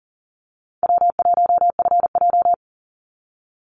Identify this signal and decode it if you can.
Morse code, heard as a keyed tone, about 40 wpm, 700 Hz. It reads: W1FJ
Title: Easy Call Signs - 40wpm